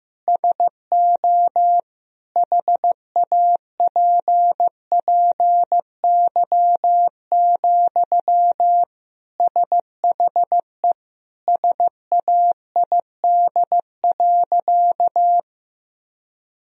Morse code, heard as a keyed tone, about 15 wpm, 700 Hz. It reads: SO HAPPY, SHE SAID.
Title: Sentences from Top 1000 Words - Review Part 1 - 15wpm